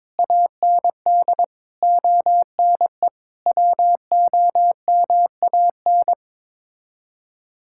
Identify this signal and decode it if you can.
Morse code, heard as a keyed tone, about 21 wpm, 700 Hz. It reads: AND ONE WOMAN
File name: practice / 22wpm / Sets of 6 Words - 22wpm